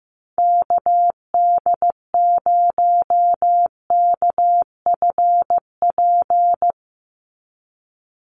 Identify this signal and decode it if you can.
Morse code, heard as a keyed tone, about 15 wpm, 700 Hz. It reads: KD0KFP
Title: USA Call Signs 15wpm